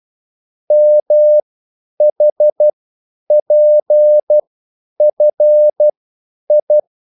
Morse code, heard as a keyed tone, about 12 wpm, spaced as for 9 wpm, 600 Hz. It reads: MHPFI
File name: QSL / CW lesson